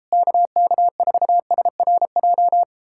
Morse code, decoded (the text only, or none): KK4SRJ